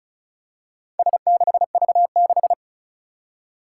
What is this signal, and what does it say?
Morse code, heard as a keyed tone, about 34 wpm, 700 Hz. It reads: S6V6
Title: Four Challenging Letter-Number Combinations 35wpm